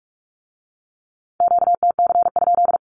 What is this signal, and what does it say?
Morse code, heard as a keyed tone, about 46 wpm, 700 Hz. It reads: QTC?
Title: Q-Codes with Spoken Meaning - 45wpm